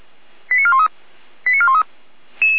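Five-tone selective call (ZVEI), heard as quick runs of five tones, tones of about 80 ms. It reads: 89413 89413
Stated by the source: ZVEI five-tone selective call received on an SDR, apparently at 455.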